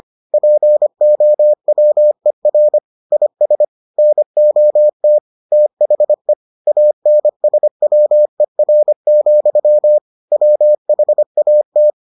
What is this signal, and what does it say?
Morse code, heard as a keyed tone, about 25 wpm, 600 Hz. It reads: POWER IS NOT THE ANSWER, WHAT